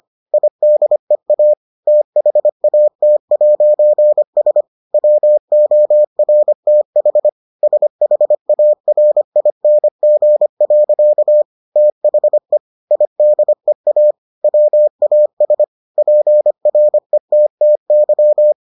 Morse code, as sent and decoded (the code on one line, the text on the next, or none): .. -.. .- / - .... .- - .----. ... / .-- --- .-. - .... / ... .... .- .-. .. -. --. .-.-.- / - .... / .. -.. .- / .-- .- ... / .--. .-. - - -.--
IDEA THAT'S WORTH SHARING. THE IDEA WAS PRETTY